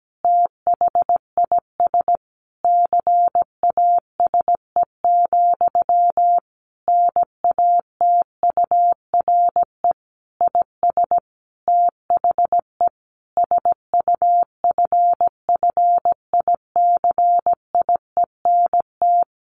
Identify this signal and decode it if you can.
Morse code, heard as a keyed tone, about 17 wpm, 700 Hz. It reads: THIS CASE, NATURE IS THE SUFFICIENT
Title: Sentences from Top 1400 Words 17wpm